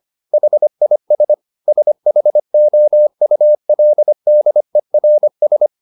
Morse code, heard as a keyed tone, about 25 wpm, 600 Hz. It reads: HIS SHOULDERS